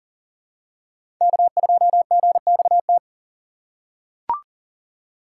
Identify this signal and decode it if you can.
Morse code, heard as a keyed tone, about 38 wpm, 700 Hz. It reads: K2GXT
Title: USA Call Signs - 5 Characters - 3x Character Spacing - 40wpm